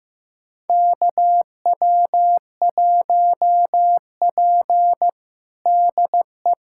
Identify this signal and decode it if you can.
Morse code, heard as a keyed tone, about 15 wpm, 700 Hz. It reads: KW1P DE
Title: USA Callsign DE Callsign - 4 Characters - 15wpm